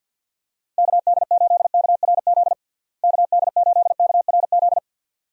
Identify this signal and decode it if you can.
Morse code, heard as a keyed tone, about 52 wpm, 700 Hz. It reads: KD8KRZ KD8KRZ
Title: USA Call Signs - 6 Characters - Repeated 2x - 50wpm